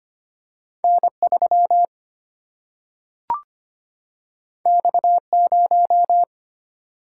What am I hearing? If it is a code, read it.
Morse code, heard as a keyed tone, about 25 wpm, 700 Hz. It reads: N 3 X 0
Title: Single-Letter Single-Number 25wpm